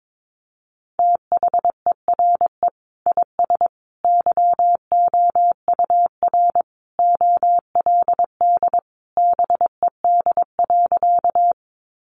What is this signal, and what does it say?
Morse code, heard as a keyed tone, about 22 wpm, 700 Hz. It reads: THERE IS YOUR OLD BED.
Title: Sentences from Top 500 Words 22wpm